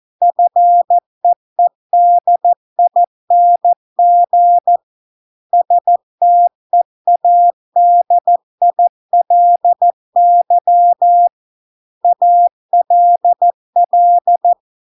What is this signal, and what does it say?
Morse code, heard as a keyed tone, about 14 wpm, 700 Hz. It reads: FEEDING STEADILY ALL